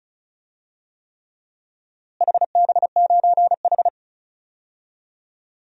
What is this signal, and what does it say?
Morse code, heard as a keyed tone, about 34 wpm, 700 Hz. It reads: HB9H
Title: DXCC-100 Call Signs - 4 Characters - Repeated 3x - 35wpm